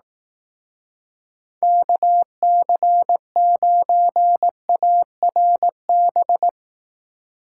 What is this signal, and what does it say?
Morse code, heard as a keyed tone, about 18 wpm, 700 Hz. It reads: KC9ARB
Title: Speed Racing - USA Call Signs 15wpm